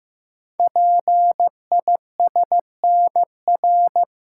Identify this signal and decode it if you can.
Morse code, heard as a keyed tone, about 15 wpm, 700 Hz. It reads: PISNR
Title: Code Groups - 5 Letters - 15wpm